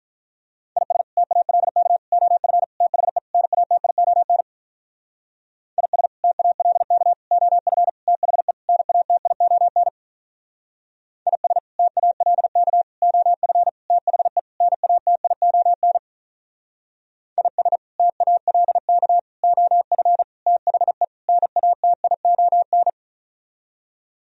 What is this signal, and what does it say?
Morse code, heard as a keyed tone, about 42 wpm, 700 Hz. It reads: IS TALK OF THE NATION IS TALK OF THE NATION IS TALK OF THE NATION IS TALK OF THE NATION
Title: Speed-Racing - Sets of 5 Words 35wpm